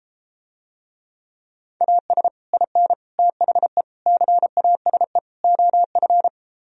Morse code, heard as a keyed tone, about 33 wpm, 700 Hz. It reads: AS IN THE CASE OF